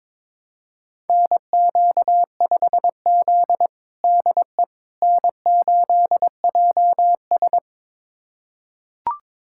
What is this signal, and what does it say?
Morse code, heard as a keyed tone, about 22 wpm, 700 Hz. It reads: NQ5Z DE N8JS
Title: USA Callsign DE Callsign - 4 Characters - 22wpm